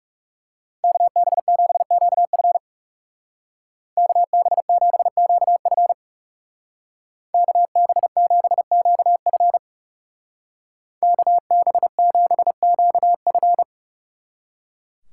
Morse code, heard as a keyed tone, about 37 wpm, 700 Hz. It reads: KB7QF KB7QF KB7QF KB7QF
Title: Speed Racing - USA Call Signs 30wpm